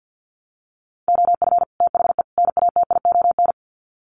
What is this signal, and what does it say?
Morse code, heard as a keyed tone, about 50 wpm, 700 Hz. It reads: OF THE NATION
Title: Sets of 3 Words 50wpm